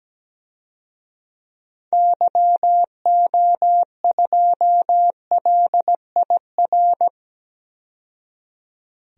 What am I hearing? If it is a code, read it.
Morse code, heard as a keyed tone, about 17 wpm, 700 Hz. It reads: YO2LIR
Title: DXCC-100 Call Signs - 6 Characters - 17wpm